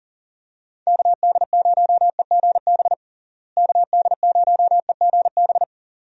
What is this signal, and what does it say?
Morse code, heard as a keyed tone, about 37 wpm, 700 Hz. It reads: KD0EGB KD0EGB
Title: USA Call Signs - 6 Characters - Repeated 2x - 40wpm